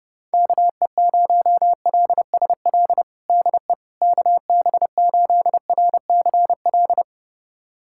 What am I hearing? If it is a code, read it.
Morse code, heard as a keyed tone, about 30 wpm, 700 Hz. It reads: KE0LSL DE KB8RCL
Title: USA Callsign DE Callsign 30wpm